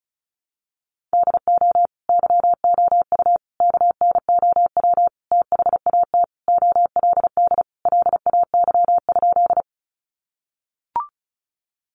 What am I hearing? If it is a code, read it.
Morse code, heard as a keyed tone, about 34 wpm, 700 Hz. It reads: DO YOU KNOW THAT OLD LAY?
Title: Sentences from Top 400 Words 35wpm